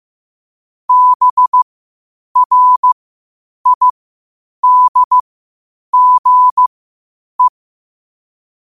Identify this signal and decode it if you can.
Morse code, heard as a keyed tone, about 15 wpm, spaced as for 8 wpm, 1000 Hz. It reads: BRIDGE
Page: New quotes every day in morse code at 8 Words per minute.